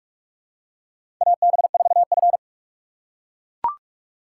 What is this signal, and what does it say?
Morse code, heard as a keyed tone, about 44 wpm, 700 Hz. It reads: AB4F
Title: USA Call Signs - 4 Characters - 4x Character Spacing - 45wpm